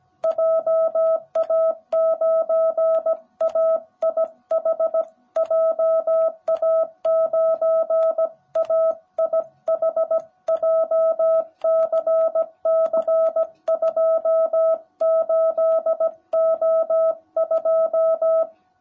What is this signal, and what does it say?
Morse code, heard as a keyed tone, about 17 wpm, 660 Hz. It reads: JA9AIH JA9AIH JCC28O2
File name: jh8cbh / 2021cw / cw18b